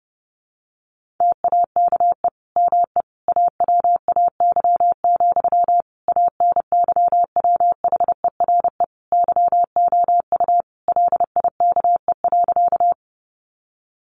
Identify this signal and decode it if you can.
Morse code, heard as a keyed tone, about 30 wpm, 700 Hz. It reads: TAKE ME AWAY, ANYWHERE YOU LIKE.
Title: Sentences from Top 1400 Words 30wpm